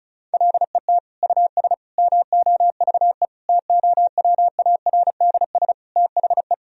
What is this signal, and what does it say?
Morse code, heard as a keyed tone, about 36 wpm, 700 Hz. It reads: LET US MOVE TOWARDS THE
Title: Sentences from Top 1000 Words - Review Part 2 - 35wpm